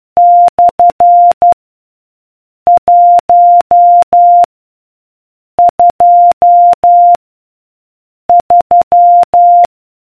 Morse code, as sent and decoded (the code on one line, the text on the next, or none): -..-. / .---- / ..--- / ...--
/ 1 2 3